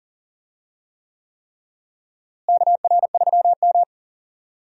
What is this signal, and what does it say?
Morse code, heard as a keyed tone, about 40 wpm, 700 Hz. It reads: KR3M
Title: USA Call Signs - 4 Characters - Repeated 3x - 40wpm